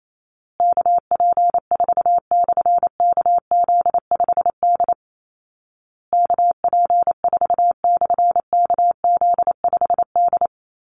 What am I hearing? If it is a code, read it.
Morse code, heard as a keyed tone, about 28 wpm, 700 Hz. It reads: KP4/KZ5D KP4/KZ5D
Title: Call Signs Repeated 2x - 28wpm